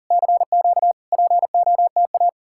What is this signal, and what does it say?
Morse code, heard as a keyed tone, about 52 wpm, 700 Hz. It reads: CQ POTA